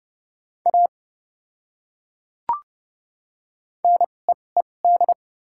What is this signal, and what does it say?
Morse code, heard as a keyed tone, about 30 wpm, spaced as for 22 wpm, 700 Hz. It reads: A NEED
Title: 2x Character Spacing - Top 500 Words - 30wpm